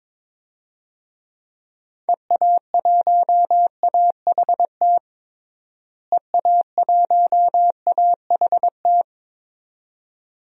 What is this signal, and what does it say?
Morse code, heard as a keyed tone, about 22 wpm, 700 Hz. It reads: EA1AHT EA1AHT
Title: DXCC-100 Call Signs - 6 Characters - Repeated 2x - 22wpm